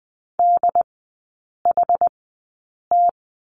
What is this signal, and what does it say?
Morse code, heard as a keyed tone, about 20 wpm, spaced as for 8 wpm, 700 Hz. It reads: DHT
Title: 093 Lesson 16e - Letter c - Three Characters 20wpm